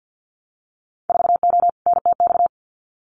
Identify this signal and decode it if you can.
Morse code, heard as a keyed tone, about 50 wpm, 700 Hz. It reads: 4O NTX
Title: Winter Field Day Exchanges 50wpm